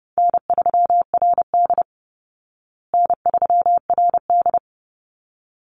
Morse code, heard as a keyed tone, about 28 wpm, 700 Hz. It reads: N3RD N3RD
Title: Easy Call Signs Repeated 2x - 30wpm